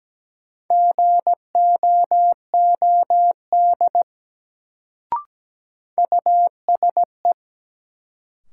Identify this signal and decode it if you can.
Morse code, heard as a keyed tone, about 17 wpm, 700 Hz. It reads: GOOD USE
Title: Sets of 2 Words - New Word Drill - 17wpm